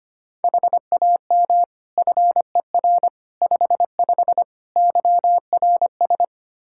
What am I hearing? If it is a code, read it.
Morse code, heard as a keyed tone, about 25 wpm, 700 Hz. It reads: HAM FER 55 YRS
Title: Practice QSO - Exchange 3 - More Info - 25wpm